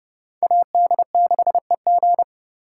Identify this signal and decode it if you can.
Morse code, heard as a keyed tone, about 37 wpm, 700 Hz. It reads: AD6EG
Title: Speed Racing - USA 2x2 Call Signs 30wpm